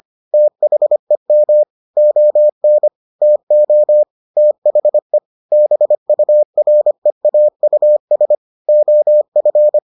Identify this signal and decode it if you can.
Morse code, heard as a keyed tone, about 25 wpm, 600 Hz. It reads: THEM ON TO THE BUREAUS OF